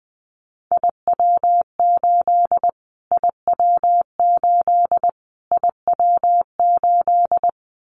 Morse code, heard as a keyed tone, about 20 wpm, 700 Hz. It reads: IW8 IW8 IW8
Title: Callsign Prefixes for Top 25 DXCC - Repeated 3x - 20wpm